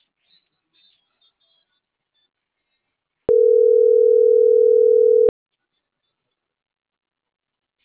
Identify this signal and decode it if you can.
Ringback tone (North America), 440 + 480 Hz.